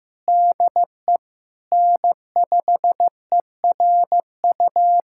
Morse code, heard as a keyed tone, about 15 wpm, 700 Hz. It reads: DE N5ERU
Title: USA Callsign DE Callsign 15wpm